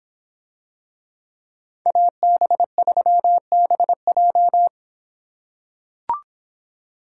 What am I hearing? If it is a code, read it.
Morse code, heard as a keyed tone, about 26 wpm, 700 Hz. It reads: AB3BJ